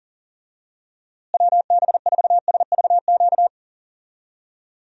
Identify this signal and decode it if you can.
Morse code, heard as a keyed tone, about 37 wpm, 700 Hz. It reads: WB4SVQ
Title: USA Call Signs - 6 Characters - Repeated 3x - 40wpm